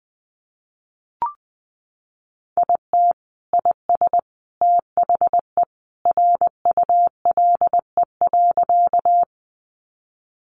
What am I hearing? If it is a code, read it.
Morse code, heard as a keyed tone, about 20 wpm, 700 Hz. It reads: IT IS THE RULE.
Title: Sentences from Top 400 Words 20wpm